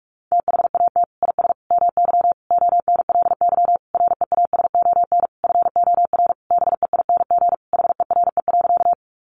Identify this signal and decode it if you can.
Morse code, heard as a keyed tone, about 44 wpm, 700 Hz. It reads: THAT IS MY ONLY REASON FOR BEING HERE.
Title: Sentences from Top 700 Words 45wpm